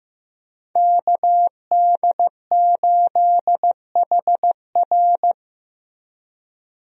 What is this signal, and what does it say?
Morse code, heard as a keyed tone, about 15 wpm, 700 Hz. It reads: KD8HR
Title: USA Call Signs - 5 Characters - Repeated 3x - 15wpm